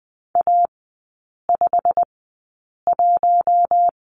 Morse code, heard as a keyed tone, about 20 wpm, spaced as for 8 wpm, 690 Hz. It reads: A51
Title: 087 Lesson 15e - Number 5 - Three Characters 20wpm